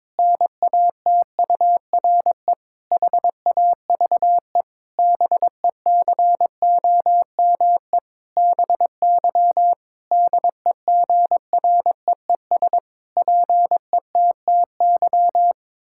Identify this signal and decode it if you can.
Morse code, heard as a keyed tone, about 22 wpm, 700 Hz. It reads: NATURE HAVE BECOME BY DEGREES PETTY
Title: Pangram Challenge v2 - 22wpm